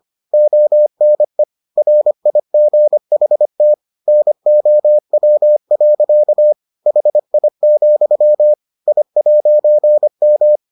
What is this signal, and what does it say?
Morse code, heard as a keyed tone, about 25 wpm, 600 Hz. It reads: ONE RIGHT NOW. HI, I'M